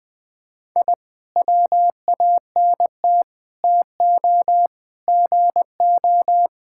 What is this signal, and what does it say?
Morse code, heard as a keyed tone, about 20 wpm, 700 Hz. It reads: I WANT TO GO
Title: Sentences with 5-Word Phrases - 20wpm